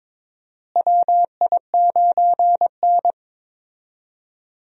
Morse code, heard as a keyed tone, about 22 wpm, 700 Hz. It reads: WI9N
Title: USA Call Signs - 4 Characters - Repeated 3x - 22wpm